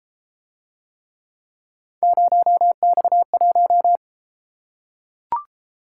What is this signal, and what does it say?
Morse code, heard as a keyed tone, about 29 wpm, 700 Hz. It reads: 0X1